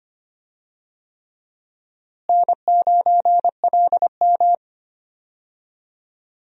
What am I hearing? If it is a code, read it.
Morse code, heard as a keyed tone, about 25 wpm, 700 Hz. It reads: N9LM
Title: USA Call Signs - 4 Characters - Repeated 3x - 25wpm